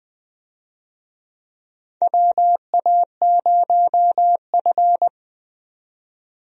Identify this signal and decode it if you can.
Morse code, heard as a keyed tone, about 20 wpm, 700 Hz. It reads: WA0F
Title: USA Call Signs - 4 Characters - 20wpm